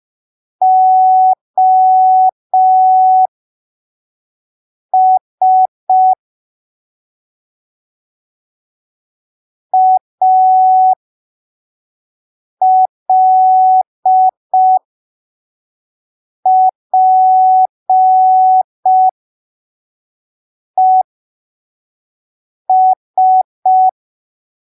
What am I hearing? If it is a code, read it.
Morse code, heard as a keyed tone, about 5 wpm, spaced as for 3 wpm, 750 Hz. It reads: OS ALPES
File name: arquivos / texto / texto 26 05ppm